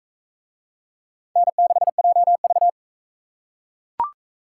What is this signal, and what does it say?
Morse code, heard as a keyed tone, about 42 wpm, 700 Hz. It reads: N6JV